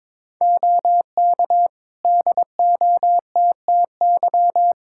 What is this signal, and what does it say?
Morse code, heard as a keyed tone, about 21 wpm, 700 Hz. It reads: OK DOTTY
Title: K1USN SST QSO Practice 22wpm